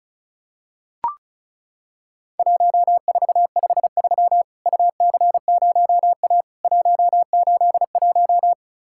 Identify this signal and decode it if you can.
Morse code, heard as a keyed tone, about 34 wpm, 700 Hz. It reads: 1453 UC0A 181
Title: Worked All Europe DX Contest - QTC Exchange - 35wpm